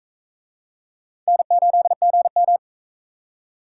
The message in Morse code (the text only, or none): N8GM